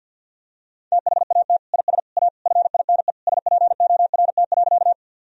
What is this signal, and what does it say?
Morse code, heard as a keyed tone, about 50 wpm, 700 Hz. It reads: THAT IS A FINE SPORT.
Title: Sentences from Top 1000 Words - Review Part 4 - 50wpm